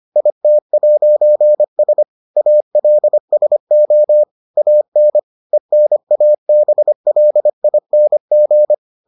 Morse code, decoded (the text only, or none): IT'S ALSO AN ENABLING